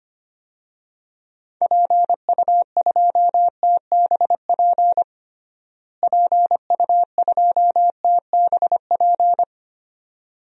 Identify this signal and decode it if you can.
Morse code, heard as a keyed tone, about 25 wpm, 700 Hz. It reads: PU2TBP PU2TBP
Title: DXCC-100 Call Signs - 6 Characters - Repeated 2x - 25wpm